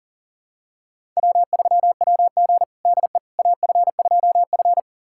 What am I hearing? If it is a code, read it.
Morse code, heard as a keyed tone, about 37 wpm, 700 Hz. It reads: W3WG DE AF2F
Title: USA Callsign DE Callsign - 4 Characters - 40wpm